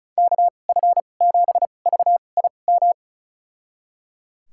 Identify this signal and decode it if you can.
Morse code, heard as a keyed tone, about 36 wpm, spaced as for 25 wpm, 700 Hz. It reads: KF7VIM
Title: USA Call Signs - 6 Characters - 2x Character Spacing - 35wpm